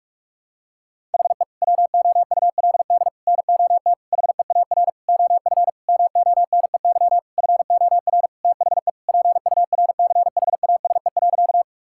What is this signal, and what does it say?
Morse code, heard as a keyed tone, about 46 wpm, 700 Hz. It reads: HE WOULD NOT HEAR OF MONEY FOR THE PURCHASE.
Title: Sentences from Top 1000 Words - Review Part 5 - 45wpm